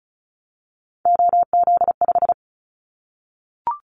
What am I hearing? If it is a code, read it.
Morse code, heard as a keyed tone, about 36 wpm, 700 Hz. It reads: OZ5
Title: Callsign Prefixes for Top 100 DXCC - 35wpm